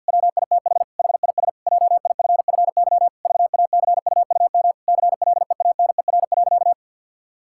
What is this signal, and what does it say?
Morse code, heard as a keyed tone, about 50 wpm, 700 Hz. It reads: WITH HIS JIFFY VACUUM CLEANER.
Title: Pangram Challenge v2 - 50wpm